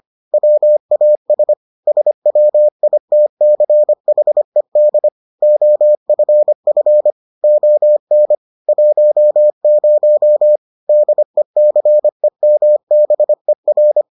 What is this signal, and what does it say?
Morse code, heard as a keyed tone, about 25 wpm, 600 Hz. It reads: WAS SWITCHED OFF ON 10 DECEMBER